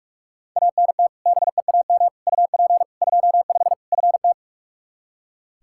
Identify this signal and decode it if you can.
Morse code, heard as a keyed tone, about 46 wpm, 700 Hz. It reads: ANT BEAM UP 25 FT
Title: K1USN SST QSO Practice 45wpm